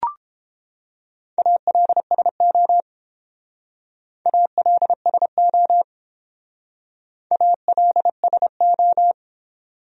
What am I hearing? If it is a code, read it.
Morse code, heard as a keyed tone, about 28 wpm, 700 Hz. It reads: ALSO ALSO ALSO